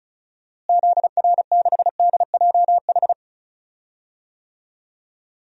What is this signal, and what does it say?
Morse code, heard as a keyed tone, about 34 wpm, 700 Hz. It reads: ZR6DJH
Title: DXCC-100 Call Signs - 6 Characters - Repeated 2x - 35wpm